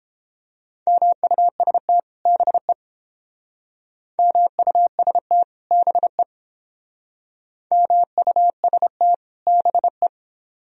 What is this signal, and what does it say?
Morse code, heard as a keyed tone, about 31 wpm, 700 Hz. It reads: MUST BE MUST BE MUST BE